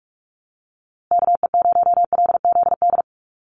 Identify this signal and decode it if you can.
Morse code, heard as a keyed tone, about 46 wpm, 700 Hz. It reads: KE0LZD
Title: USA Call Signs Repeated 2x 45wpm